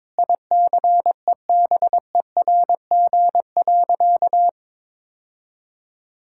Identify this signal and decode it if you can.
Morse code, heard as a keyed tone, about 22 wpm, 700 Hz. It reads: ICEBERG.
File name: Sentences with 7-Word Phrases - 22wpm